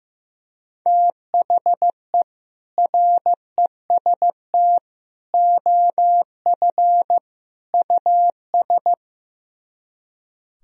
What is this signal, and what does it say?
Morse code, heard as a keyed tone, about 15 wpm, 700 Hz. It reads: THE REST OF US
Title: Speed-Racing - Sets of 4 Words 15wpm